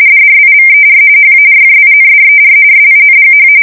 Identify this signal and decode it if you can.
RTTY, about 45.45 baud. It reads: DRIVER HARJINDER SINGH
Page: 170 Shift 2125/2295